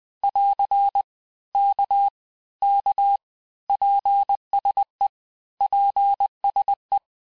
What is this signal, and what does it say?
Morse code, heard as a keyed tone, about 20 wpm, 800 Hz. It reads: <AR> K K PSE PSE
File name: cours-du-lundi files / cours-1584-texte-abreviations-vitesse-19mpm-puis-24mpm